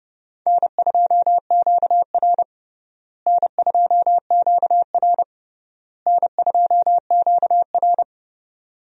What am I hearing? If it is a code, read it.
Morse code, heard as a keyed tone, about 30 wpm, 700 Hz. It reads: N2QR N2QR N2QR
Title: USA Call Signs - 4 Characters - Repeated 3x - 30wpm